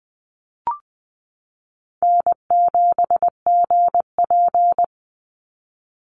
Morse code, heard as a keyed tone, about 20 wpm, 700 Hz. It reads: N7GP
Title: USA Call Signs 20wpm